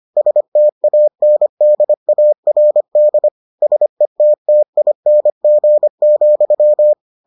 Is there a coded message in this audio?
Yes: Morse code from 0.2 to 6.9 s, heard as a keyed tone, about 25 wpm, 600 Hz. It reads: STANDARD SETTING,